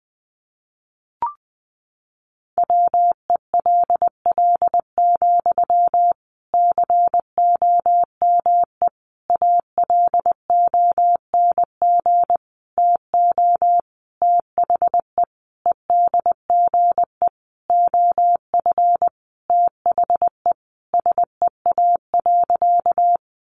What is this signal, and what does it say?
Morse code, heard as a keyed tone, about 20 wpm, 700 Hz. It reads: WELL, COME ALONG TO THE EDGE OF THE SEA.
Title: Sentences from Top 1300 Words 20wpm